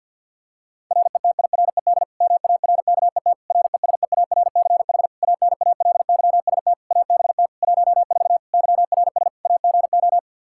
Morse code, heard as a keyed tone, about 52 wpm, 700 Hz. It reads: RETIRED MARKET RESEARCH ANALYST ABT 14 YRS AGO
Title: Practice QSO - Exchange 3 - More Info - 50wpm